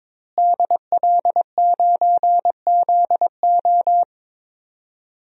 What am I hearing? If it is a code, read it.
Morse code, heard as a keyed tone, about 22 wpm, 700 Hz. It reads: DL9ZO
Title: DXCC-100 Call Signs - 5 Characters - Repeated 2x - 22wpm